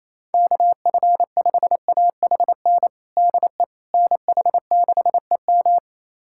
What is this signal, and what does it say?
Morse code, heard as a keyed tone, about 27 wpm, 700 Hz. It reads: KF5AHN DE NH6EM
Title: USA Callsign DE Callsign 28wpm